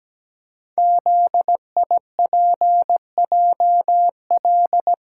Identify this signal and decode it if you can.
Morse code, heard as a keyed tone, about 17 wpm, 700 Hz. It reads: ZIPJL
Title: Code Groups - 5 Letter:Numbers - 17wpm